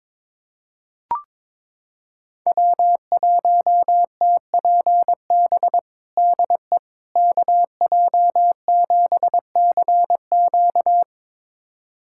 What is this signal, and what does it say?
Morse code, heard as a keyed tone, about 22 wpm, 700 Hz. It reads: W1TPB DE KJ7CQ
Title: USA Callsign DE Callsign - 5 Characters - 22wpm